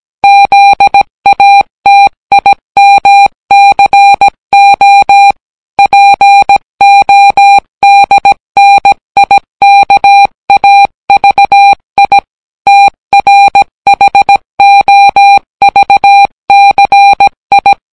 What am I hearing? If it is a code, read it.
Morse code, heard as a keyed tone, about 17 wpm, 800 Hz. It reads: ZATIMCO PODNIKAVI TRHOVCI